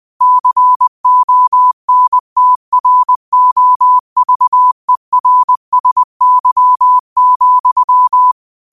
Morse code, heard as a keyed tone, about 20 wpm, 1000 Hz. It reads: CONTROVERSY,
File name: qotdfiles / 20wpm